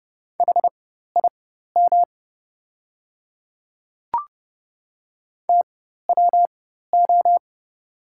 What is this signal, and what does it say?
Morse code, heard as a keyed tone, about 30 wpm, spaced as for 14 wpm, 700 Hz. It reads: HIM TWO
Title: 4x Character Spacing - Top 100 Words - 30wpm